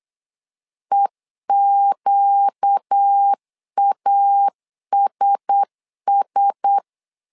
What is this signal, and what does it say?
Morse code, heard as a keyed tone, about 8 wpm, 800 Hz. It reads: EQASS
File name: LEZIONI / 4 Q1S 8wpm 2710